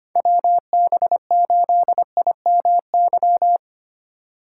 Morse code, heard as a keyed tone, about 25 wpm, 700 Hz. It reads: WB8IMY
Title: K1USN SST QSO Practice - 25wpm